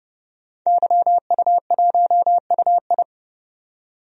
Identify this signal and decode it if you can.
Morse code, heard as a keyed tone, about 30 wpm, 700 Hz. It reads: YU1UI
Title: DXCC-100 Call Signs - 5 Characters - 30wpm